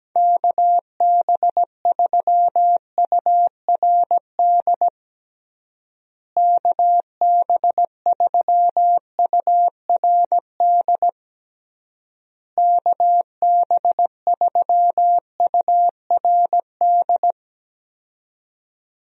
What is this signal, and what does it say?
Morse code, heard as a keyed tone, about 17 wpm, 700 Hz. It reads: KB3URD KB3URD KB3URD
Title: USA Call Signs - 6 Characters - Repeated 3x - 17wpm